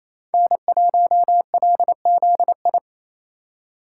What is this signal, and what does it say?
Morse code, heard as a keyed tone, about 27 wpm, 700 Hz. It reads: N1LZI
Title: USA Call Signs - 2x Character Spacing - 28wpm